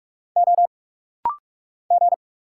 Morse code, heard as a keyed tone, about 46 wpm, 700 Hz. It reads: O G
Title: Single Letters - Rapid-Fire 45wpm